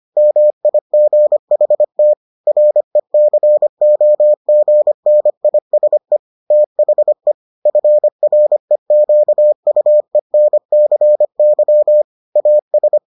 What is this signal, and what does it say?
Morse code, heard as a keyed tone, about 25 wpm, 600 Hz. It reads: MIGHT RECOGNISE THE FREQUENCY AS